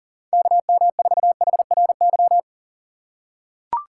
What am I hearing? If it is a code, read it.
Morse code, heard as a keyed tone, about 40 wpm, 700 Hz. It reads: KM4HRY
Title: USA Call Signs - 6 Characters - 2x Character Spacing - 40wpm